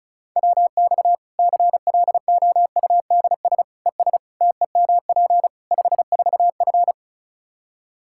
Morse code, heard as a keyed tone, about 36 wpm, 700 Hz. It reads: WX CLOUDS ES TEMP 54F
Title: Practice QSO - Exchange 2 - Info - 35wpm